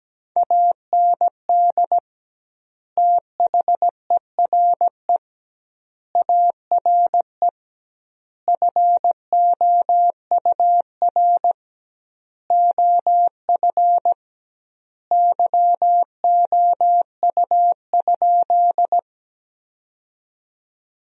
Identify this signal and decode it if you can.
Morse code, heard as a keyed tone, about 17 wpm, 700 Hz. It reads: AND THERE ARE FOUR OF YOU?
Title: 2x Word-Spacing - Sentences from Top 300 Words - 17wpm